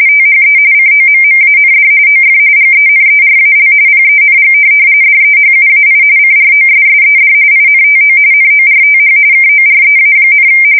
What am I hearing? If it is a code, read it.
RTTY, about 45.45 baud. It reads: THE COMMAND OF GENERAL PENG DEHUAI CROSSED INTO NORTH KOREA,
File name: fsk170 / USS-Missouri